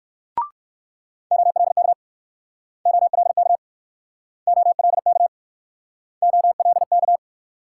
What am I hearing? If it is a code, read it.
Morse code, heard as a keyed tone, about 61 wpm, 700 Hz. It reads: OLK OLK OLK OLK